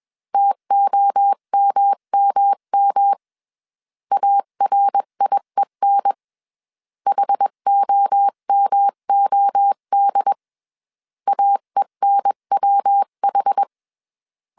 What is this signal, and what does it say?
Morse code, heard as a keyed tone, about 21 wpm, 800 Hz. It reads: TOMMM ARIEN HOMOD AENWH
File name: 20wpm